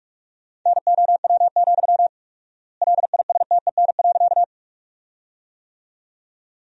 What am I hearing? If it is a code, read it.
Morse code, heard as a keyed tone, about 46 wpm, 700 Hz. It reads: NOW, LISTEN.
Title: 4x Word-Spacing - Sentences from Top 400 Words - 45wpm